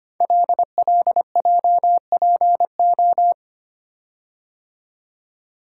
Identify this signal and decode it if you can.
Morse code, heard as a keyed tone, about 25 wpm, 700 Hz. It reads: LLJPO
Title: Code Groups - 5 Letters - 25wpm